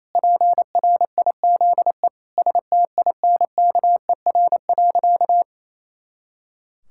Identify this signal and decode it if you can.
Morse code, heard as a keyed tone, about 28 wpm, 700 Hz. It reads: PRIZE STINKER.
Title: Pangram Challenge v2 - 28wpm